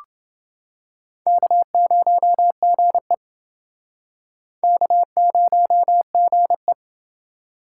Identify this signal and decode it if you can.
Morse code, heard as a keyed tone, about 29 wpm, 700 Hz. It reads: K0GE K0GE